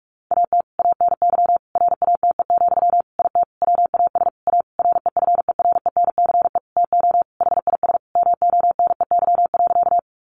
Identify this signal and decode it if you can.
Morse code, heard as a keyed tone, about 44 wpm, 700 Hz. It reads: AT ANY RATE, IT WAS A REFERENCE TO HIS MONEY.
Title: Sentences from Top 1200 Words 45wpm